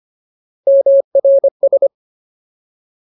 Morse code, heard as a keyed tone, about 25 wpm, 550 Hz. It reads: MRS
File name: Ultimate Word List / 25WPM Ultimate